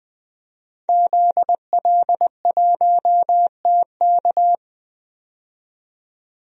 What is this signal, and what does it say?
Morse code, heard as a keyed tone, about 20 wpm, 700 Hz. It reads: ZL1TK
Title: DXCC-100 Call Signs - 5 Characters - 20wpm